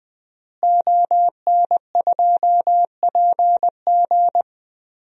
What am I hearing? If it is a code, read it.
Morse code, heard as a keyed tone, about 21 wpm, 700 Hz. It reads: ON2PG
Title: DXCC-100 Call Signs - 5 Characters - Repeated 3x - 20wpm